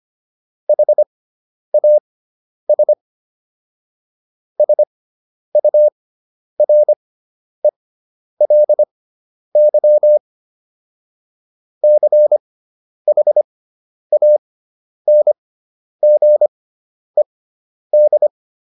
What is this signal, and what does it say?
Morse code, heard as a keyed tone, about 25 wpm, spaced as for 10 wpm, 600 Hz. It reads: HAS SURELY CHANGED